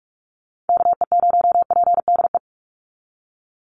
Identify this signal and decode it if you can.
Morse code, heard as a keyed tone, about 44 wpm, 700 Hz. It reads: KE0PDE
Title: USA Call Signs 45wpm